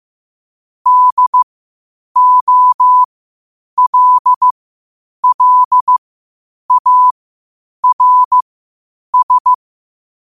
Morse code, heard as a keyed tone, about 15 wpm, spaced as for 8 wpm, 1000 Hz. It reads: DOLLARS